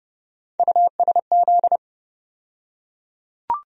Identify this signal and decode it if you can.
Morse code, heard as a keyed tone, about 30 wpm, 700 Hz. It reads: USZ
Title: USA Call Sign Postfixes - 3 Letters 30wpm